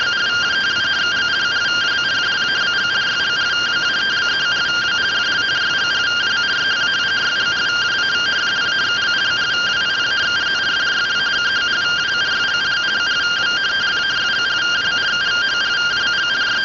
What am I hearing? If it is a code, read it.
Two-tone teleprinter signal, about 170 Hz shift.